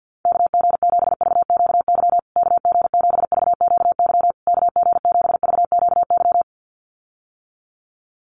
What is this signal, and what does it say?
Morse code, heard as a keyed tone, about 50 wpm, 700 Hz. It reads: KG7VQY KG7VQY KG7VQY
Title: USA Call Signs Repeated 3x 50wpm